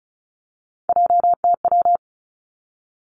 Morse code, heard as a keyed tone, about 36 wpm, 700 Hz. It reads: JTW
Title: Three Challenging Letter-Number Combinations 35wpm